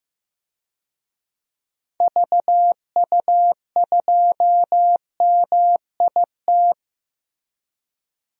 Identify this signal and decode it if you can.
Morse code, heard as a keyed tone, about 15 wpm, 700 Hz. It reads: VU2MIT
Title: DXCC-100 Call Signs - 6 Characters - Repeated 3x - 15wpm